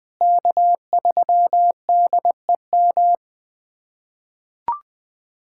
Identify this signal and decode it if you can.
Morse code, heard as a keyed tone, about 21 wpm, 700 Hz. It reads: K3DEM
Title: USA Callsign DE Callsign - 5 Characters - 20wpm